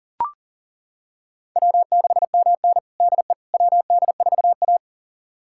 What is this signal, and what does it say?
Morse code, heard as a keyed tone, about 40 wpm, 700 Hz. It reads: W6MN DE WD4A
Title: USA Callsign DE Callsign - 4 Characters - 40wpm